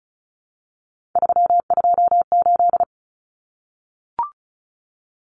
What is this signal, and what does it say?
Morse code, heard as a keyed tone, about 34 wpm, 700 Hz. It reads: 328
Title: Common Cut Numbers - 3-Digit 35wpm